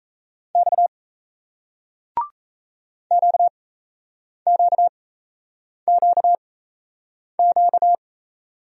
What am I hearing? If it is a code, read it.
Morse code, heard as a keyed tone, about 38 wpm, spaced as for 8 wpm, 700 Hz. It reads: X QQQQ